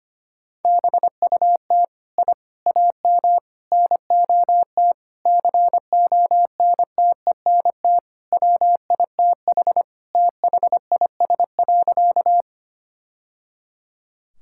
Morse code, heard as a keyed tone, about 25 wpm, 700 Hz. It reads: BUT I AM NOT CONTENT WITH THIS.
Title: Sentences from Top 1000 Words - Review Part 6 - 25wpm